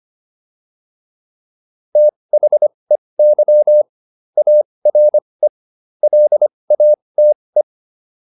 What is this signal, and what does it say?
Morse code, heard as a keyed tone, about 25 wpm, spaced as for 20 wpm, 600 Hz. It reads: THEY ARE LATE